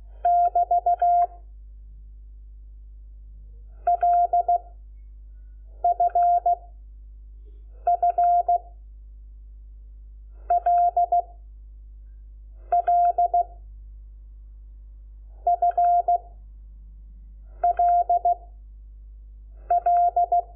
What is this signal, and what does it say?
Morse code, heard as a keyed tone, about 16 wpm, 700 Hz. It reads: <BT> L F F L L F L L
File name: Lecciones / CLASE Nº5